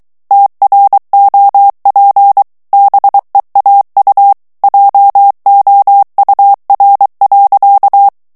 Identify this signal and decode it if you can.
Morse code, heard as a keyed tone, about 23 wpm, 800 Hz. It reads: TROP BEAU JOUR.
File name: morse mp3 / FRANCAIS TEXTE CLAIR 1 VITESSE 22 MOTS MINUTES